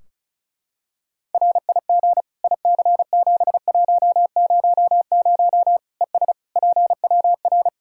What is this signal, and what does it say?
Morse code, heard as a keyed tone, about 34 wpm, 700 Hz. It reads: RIG IC7100 ES PWR
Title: Practice QSO - Exchange 2 - Info - 35wpm